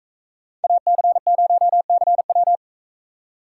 Morse code, heard as a keyed tone, about 39 wpm, 700 Hz. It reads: AC0CW